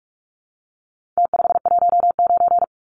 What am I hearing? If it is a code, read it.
Morse code, heard as a keyed tone, about 47 wpm, 700 Hz. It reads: T519
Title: Numbers - Contest Style - 4-Digit 45wpm